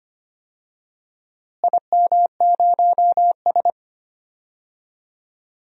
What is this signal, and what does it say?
Morse code, heard as a keyed tone, about 25 wpm, 700 Hz. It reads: IM0S
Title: DXCC-100 Call Signs - 4 Characters - Repeated 2x - 25wpm